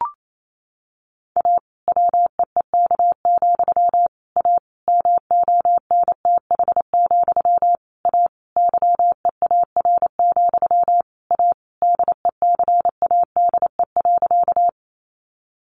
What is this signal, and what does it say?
Morse code, heard as a keyed tone, about 28 wpm, 700 Hz. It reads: A WEEK, A MONTH, A YEAR, A DECADE.
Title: A Sound Of Thunder 28wpm